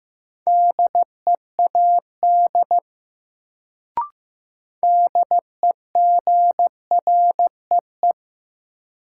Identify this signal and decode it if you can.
Morse code, heard as a keyed tone, about 15 wpm, 700 Hz. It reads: DEAD DEGREE
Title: Sentences from Top 1500 Words - New Word Drill - 15wpm